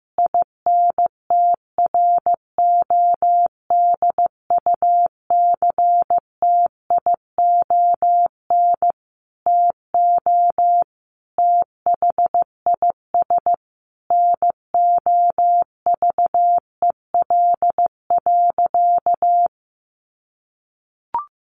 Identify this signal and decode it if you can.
Morse code, heard as a keyed tone, about 14 wpm, 700 Hz. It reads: INTRODUCTION TO THIS NOVEL.
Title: Sentences from Top 1300 Words 15wpm